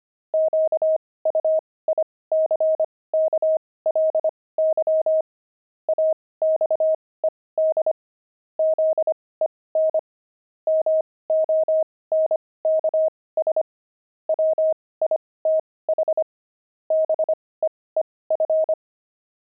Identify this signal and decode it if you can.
Morse code, heard as a keyed tone, about 25 wpm, spaced as for 18 wpm, 630 Hz. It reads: QUICKLY AXED ZEN MONKS WITH BEEF
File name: pangram-fs-18-cs-25-monks